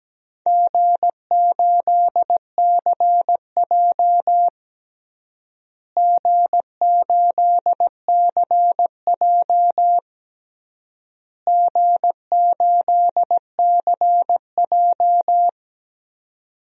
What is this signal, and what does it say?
Morse code, heard as a keyed tone, about 17 wpm, 700 Hz. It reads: G8CJ G8CJ G8CJ
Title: DXCC-100 Call Signs - 4 Characters - Repeated 3x - 17wpm